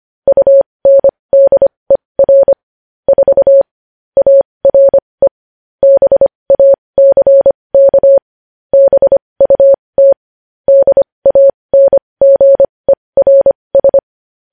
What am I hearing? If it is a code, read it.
Morse code, heard as a keyed tone, about 25 wpm, spaced as for 20 wpm, 550 Hz. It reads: UNDER 4 ARE BACK BUT DANGERS
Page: news headlines in morse code at 20 wpm